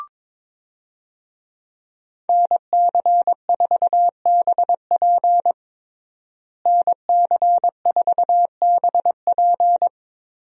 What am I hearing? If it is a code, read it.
Morse code, heard as a keyed tone, about 22 wpm, 700 Hz. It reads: NC4BP NC4BP
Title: USA Call Signs - 5 Characters - Repeated 2x - 22wpm